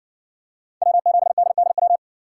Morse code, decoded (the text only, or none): W7DDW